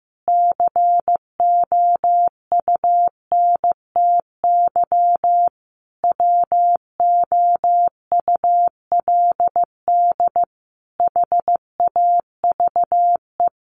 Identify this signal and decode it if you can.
Morse code, heard as a keyed tone, about 15 wpm, 700 Hz. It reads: COUNTY WOULD HAVE
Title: Sentences from Top 1200 Words 15wpm